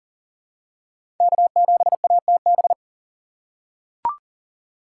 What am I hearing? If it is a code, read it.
Morse code, heard as a keyed tone, about 40 wpm, 700 Hz. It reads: K7ATB
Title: USA Call Signs - 5 Characters - 2x Character Spacing - 40wpm